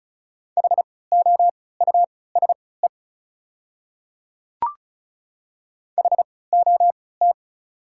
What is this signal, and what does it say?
Morse code, heard as a keyed tone, about 34 wpm, spaced as for 20 wpm, 700 Hz. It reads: HOUSE HOT
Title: 3x Character Spacing - Top 300 Words - 35wpm